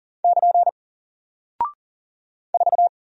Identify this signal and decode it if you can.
Morse code, heard as a keyed tone, about 40 wpm, 700 Hz. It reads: <KN> 4
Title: Single Character (Letter:Number:Punctuation:Prosign) Alt 40wpm